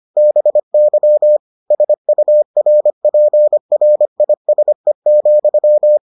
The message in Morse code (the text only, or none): BY SURPRISE,